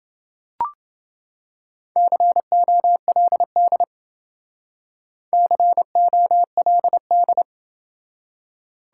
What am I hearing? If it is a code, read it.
Morse code, heard as a keyed tone, about 29 wpm, 700 Hz. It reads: COLD COLD